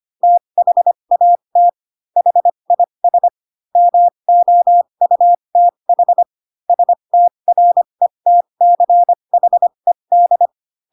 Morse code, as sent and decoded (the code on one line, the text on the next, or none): - .... .- - / .... .. ... / -- --- ..- - .... / ... - .-. - -.-. .... -..
THAT HIS MOUTH STRETCHED